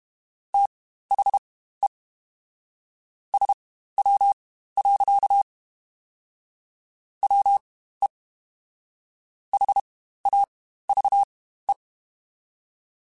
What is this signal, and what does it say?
Morse code, heard as a keyed tone, about 32 wpm, spaced as for 15 wpm, 800 Hz. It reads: THE SW. WE HAVE
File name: Spaced Characters / Spaced Characters-mp3s / Weather 5 5 sp @30